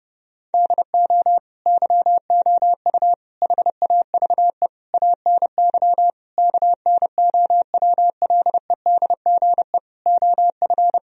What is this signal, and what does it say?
Morse code, heard as a keyed tone, about 32 wpm, 700 Hz. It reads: DO YOU HAVE ANY KNOWLEDGE OF
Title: Sentences from Top 1000 Words - Review Part 3 - 30wpm